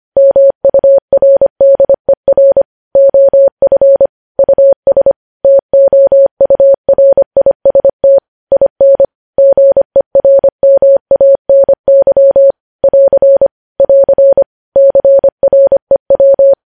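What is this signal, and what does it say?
Morse code, heard as a keyed tone, about 25 wpm, 550 Hz. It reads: MURDER OF US TOURIST IN GERMANY <AR> <AR> CREW
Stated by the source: Morse code news headlines sent at 25 wpm.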